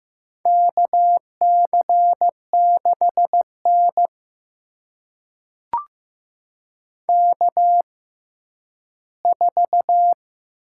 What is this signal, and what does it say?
Morse code, heard as a keyed tone, about 15 wpm, spaced as for 10 wpm, 700 Hz. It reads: KC6N K4
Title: USA Call Signs - 4 Characters - 6x Character Spacing - 15wpm